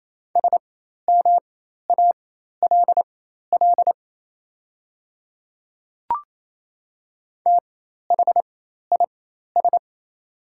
Morse code, heard as a keyed tone, about 28 wpm, spaced as for 13 wpm, 700 Hz. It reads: SMALL THIS
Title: 4x Character Spacing - Top 200 Words - 28wpm